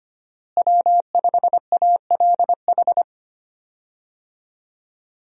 Morse code, heard as a keyed tone, about 25 wpm, 700 Hz. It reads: W5ALH
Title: USA Call Signs - 5 Characters - Repeated 3x - 25wpm